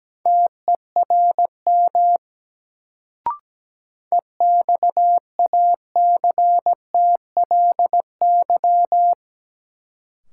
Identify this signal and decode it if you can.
Morse code, heard as a keyed tone, about 17 wpm, 700 Hz. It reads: TERM EXACTLY
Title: Sentences from Top 900 Words - New Word Drill - 17wpm